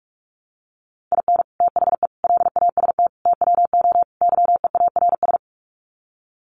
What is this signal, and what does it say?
Morse code, heard as a keyed tone, about 44 wpm, 700 Hz. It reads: IN THE LAST TWO YEARS
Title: Sets of 5 Words 45wpm